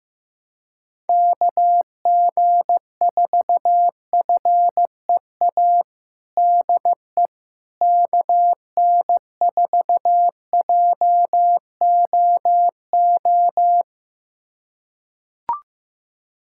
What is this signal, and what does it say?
Morse code, heard as a keyed tone, about 16 wpm, 700 Hz. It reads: KG4FEA DE KN4JOO
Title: USA Callsign DE Callsign - 6 Characters - 15wpm